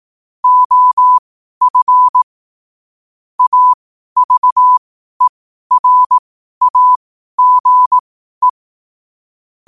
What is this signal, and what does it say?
Morse code, heard as a keyed tone, about 18 wpm, spaced as for 12 wpm, 1000 Hz. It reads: OF AVERAGE
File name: qotdfiles / 12wpm